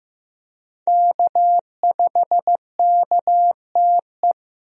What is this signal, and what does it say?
Morse code, heard as a keyed tone, about 15 wpm, 700 Hz. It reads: K5KTE
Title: USA Call Signs - 5 Characters - Repeated 3x - 15wpm